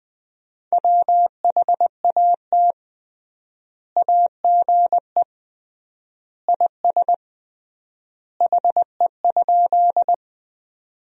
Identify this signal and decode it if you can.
Morse code, heard as a keyed tone, about 20 wpm, 700 Hz. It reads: WHAT AGE IS HE?
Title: 3x Word-Spacing - Sentences from Top 500 Words - 20wpm